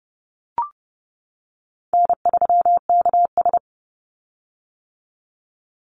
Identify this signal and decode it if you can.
Morse code, heard as a keyed tone, about 30 wpm, 700 Hz. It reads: N3KS
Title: Easy Call Signs - 30wpm